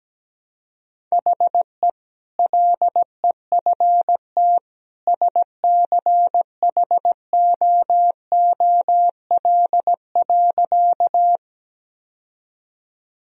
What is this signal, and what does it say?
Morse code, heard as a keyed tone, about 17 wpm, 700 Hz. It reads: HE LEFT SCHOOL.
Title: Sentences from Top 1000 Words - Review Part 1 - 17wpm